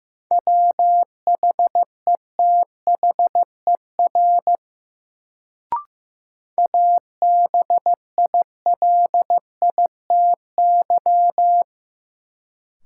Morse code, heard as a keyed tone, about 15 wpm, 700 Hz. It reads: WHETHER ABILITY
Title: Sets of 3 Words - New Word Drill - 15wpm